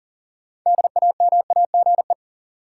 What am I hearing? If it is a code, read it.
Morse code, heard as a keyed tone, about 38 wpm, 700 Hz. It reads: DAMAGE
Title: Sentences from Top 1100 Words - New Word Drill - 40wpm